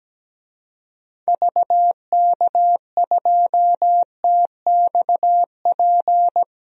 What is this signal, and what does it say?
Morse code, heard as a keyed tone, about 17 wpm, 700 Hz. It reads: VK2TXP
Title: DXCC-100 Call Signs - 6 Characters - 17wpm